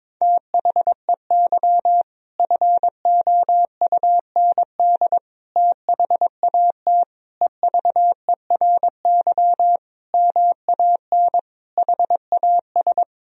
Morse code, heard as a keyed tone, about 22 wpm, 700 Hz. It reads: THEY FOUND THAT EVERY MAN HAS
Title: Sentences from Top 1000 Words - Review Part 4 - 22wpm